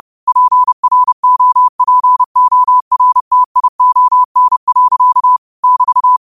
Morse code, heard as a keyed tone, about 30 wpm, 1000 Hz. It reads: PROPORTION. <BT>
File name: qotdfiles / 30wpm